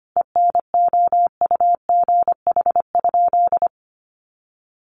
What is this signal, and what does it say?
Morse code, heard as a keyed tone, about 25 wpm, 700 Hz. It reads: ENOUGH?
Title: Sentences from Top 1200 Words 25wpm